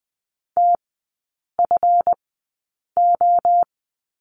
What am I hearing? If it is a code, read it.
Morse code, heard as a keyed tone, about 20 wpm, spaced as for 8 wpm, 700 Hz. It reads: TFO
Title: 138 Lesson 23e - Letter f - Three Characters 20wpm